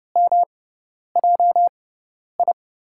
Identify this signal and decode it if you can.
Morse code, heard as a keyed tone, about 29 wpm, spaced as for 10 wpm, 700 Hz. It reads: MJI
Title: USA Call Signs - 6 Characters - 6x Character Spacing - 30wpm